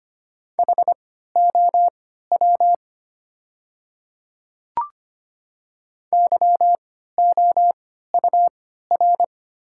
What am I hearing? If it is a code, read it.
Morse code, heard as a keyed tone, about 25 wpm, spaced as for 14 wpm, 700 Hz. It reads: HOW YOUR
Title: 3x Character Spacing - Top 100 Words - 25wpm